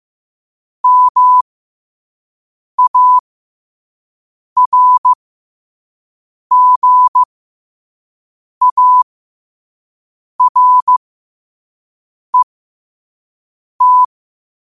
Morse code, heard as a keyed tone, about 15 wpm, spaced as for 5 wpm, 1000 Hz. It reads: MARGARET
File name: qotdfiles / 5wpm